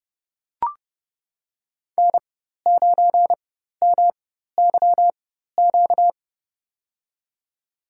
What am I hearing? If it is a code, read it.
Morse code, heard as a keyed tone, about 31 wpm, spaced as for 14 wpm, 700 Hz. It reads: N9MYQ
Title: USA Call Signs - 5 Characters - 4x Character Spacing - 30wpm